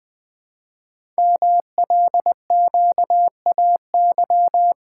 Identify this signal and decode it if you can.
Morse code, heard as a keyed tone, about 20 wpm, 700 Hz. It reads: MLQAY
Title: Code Groups - 5 Letters - 20wpm